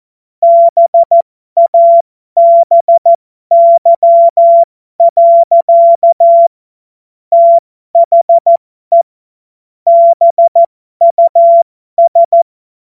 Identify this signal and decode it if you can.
Morse code, heard as a keyed tone, about 14 wpm, spaced as for 12 wpm, 680 Hz. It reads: BABY. THE BUS
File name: Morsekurs / Witze